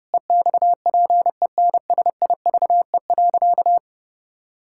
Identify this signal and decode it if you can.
Morse code, heard as a keyed tone, about 32 wpm, 700 Hz. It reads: EXPENSIVE.
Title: Sentences from Top 1000 Words - Review Part 6 - 30wpm